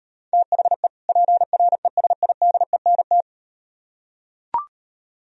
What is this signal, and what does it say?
Morse code, heard as a keyed tone, about 38 wpm, 700 Hz. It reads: THE PRESIDENT